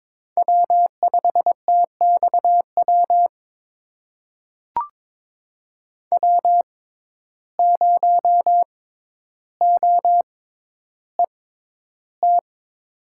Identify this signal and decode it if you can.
Morse code, heard as a keyed tone, about 22 wpm, spaced as for 11 wpm, 700 Hz. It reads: W5TXW W0OET
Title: USA Call Signs - 5 Characters - 6x Character Spacing - 22wpm